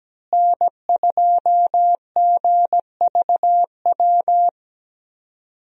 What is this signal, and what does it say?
Morse code, heard as a keyed tone, about 17 wpm, 700 Hz. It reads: N2GVW
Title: USA Call Signs - 5 Characters - Repeated 3x - 17wpm